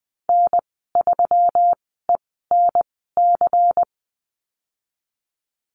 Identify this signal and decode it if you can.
Morse code, heard as a keyed tone, about 20 wpm, spaced as for 15 wpm, 700 Hz. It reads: N3ENC
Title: 179 Lesson 29g - Slash - Call Signs 20wpm